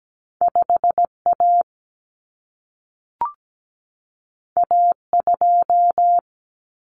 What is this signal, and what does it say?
Morse code, heard as a keyed tone, about 17 wpm, 700 Hz. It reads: 5A A2
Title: Common Cut Numbers - 2-Digit 17wpm